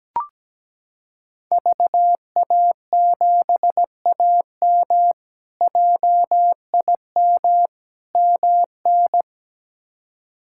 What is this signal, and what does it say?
Morse code, heard as a keyed tone, about 17 wpm, 700 Hz. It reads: VA7AM JIM MN
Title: K1USN SST QSO Practice - 17wpm